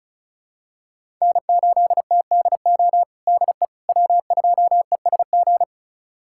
Morse code, heard as a keyed tone, about 34 wpm, 700 Hz. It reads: N8TDO DE W2ESG
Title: USA Callsign DE Callsign - 5 Characters - 35wpm